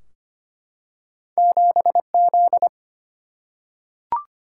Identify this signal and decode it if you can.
Morse code, heard as a keyed tone, about 25 wpm, 700 Hz. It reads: 7 Z
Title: Single-Letter Single-Number 25wpm